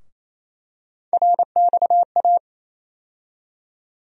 Morse code, heard as a keyed tone, about 28 wpm, 700 Hz. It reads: RXA
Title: USA Call Sign Postfixes - 3 Letters 28wpm